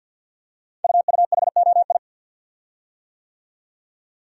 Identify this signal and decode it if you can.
Morse code, heard as a keyed tone, about 52 wpm, 700 Hz. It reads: UUHOI
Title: Code Groups - 5 Letters - 50wpm